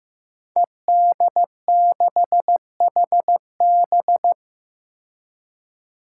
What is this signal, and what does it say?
Morse code, heard as a keyed tone, about 15 wpm, 700 Hz. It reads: ED6HB
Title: Five Challenging Letter-Number Combinations 15wpm